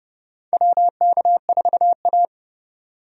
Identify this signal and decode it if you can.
Morse code, heard as a keyed tone, about 30 wpm, 700 Hz. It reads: WK4A
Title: USA Call Signs - 4 Characters - 30wpm